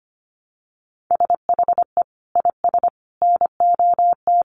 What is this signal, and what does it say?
Morse code, heard as a keyed tone, about 25 wpm, 700 Hz. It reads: SHE IS NOT
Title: Sentences from Top 700 Words 25wpm